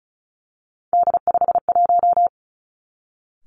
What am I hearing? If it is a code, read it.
Morse code, heard as a keyed tone, about 36 wpm, 700 Hz. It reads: D51
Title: Three Challenging Letter-Number Combinations 35wpm